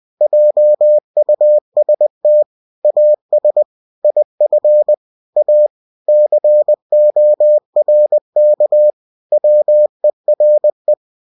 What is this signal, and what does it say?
Morse code, heard as a keyed tone, about 20 wpm, 600 Hz. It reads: JUST AS IF A CORK WERE
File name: cw-books / winnie-the-pooh / winnie-20wpm-page-0019